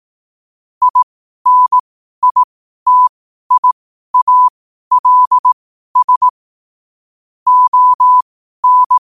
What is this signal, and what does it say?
Morse code, heard as a keyed tone, about 18 wpm, spaced as for 12 wpm, 1000 Hz. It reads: INITIALS ON
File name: qotdfiles / 12wpm